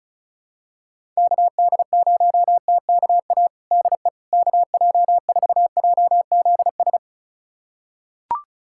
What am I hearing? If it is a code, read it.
Morse code, heard as a keyed tone, about 36 wpm, 700 Hz. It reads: KD0TKA DE KJ4JZS
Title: USA Callsign DE Callsign - 6 Characters - 35wpm